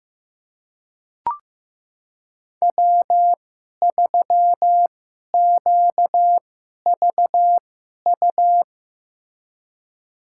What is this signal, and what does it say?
Morse code, heard as a keyed tone, about 15 wpm, spaced as for 11 wpm, 700 Hz. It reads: W3QVU
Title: USA Call Signs - 5 Characters - 2x Character Spacing - 15wpm